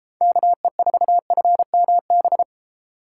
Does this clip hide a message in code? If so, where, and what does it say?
Yes: Morse code from 0.2 to 2.4 s, heard as a keyed tone, about 33 wpm, 700 Hz. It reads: KE4FMB